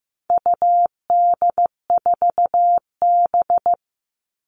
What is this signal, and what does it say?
Morse code, heard as a keyed tone, about 15 wpm, 700 Hz. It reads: UD4B
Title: All Cut Numbers - 4-Digit 15wpm